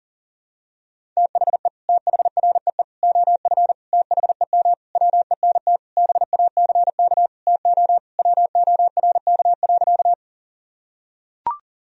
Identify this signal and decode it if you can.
Morse code, heard as a keyed tone, about 40 wpm, 700 Hz. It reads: THE THREE OF THEM WENT BACK TO WORK.
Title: Sentences from Top 1000 Words - Review Part 1 - 40wpm